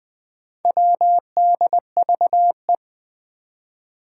Morse code, heard as a keyed tone, about 20 wpm, 700 Hz. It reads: WDVE
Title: Four Challenging Letter-Number Combinations 20wpm